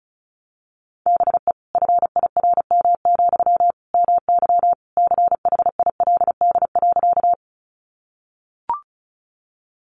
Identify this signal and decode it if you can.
Morse code, heard as a keyed tone, about 34 wpm, 700 Hz. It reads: BE FIRM, MY CHILD.
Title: Sentences from Top 1400 Words 35wpm